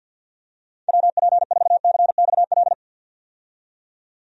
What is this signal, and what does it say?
Morse code, heard as a keyed tone, about 50 wpm, 700 Hz. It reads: WP4CXL
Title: USA Call Signs - 6 Characters - 50wpm